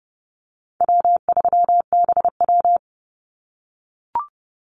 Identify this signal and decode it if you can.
Morse code, heard as a keyed tone, about 30 wpm, 700 Hz. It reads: W3BW
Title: Easy Call Signs - 30wpm